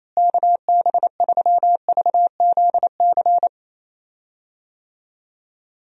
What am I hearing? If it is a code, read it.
Morse code, heard as a keyed tone, about 28 wpm, 700 Hz. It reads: KB3VZC
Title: USA Call Signs - 6 Characters - Repeated 3x - 28wpm